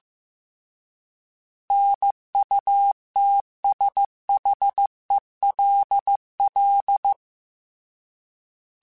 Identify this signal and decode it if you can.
Morse code, heard as a keyed tone, about 15 wpm, 800 Hz. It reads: NUTSHELL
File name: Alphabetical / Alphabetical mp3s / Ns @13